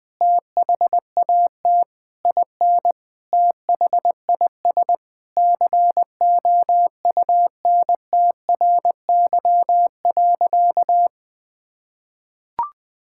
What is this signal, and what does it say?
Morse code, heard as a keyed tone, about 20 wpm, 700 Hz. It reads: THAT IN THIS COUNTRY.
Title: Sentences from Top 1000 Words - Review Part 5 - 20wpm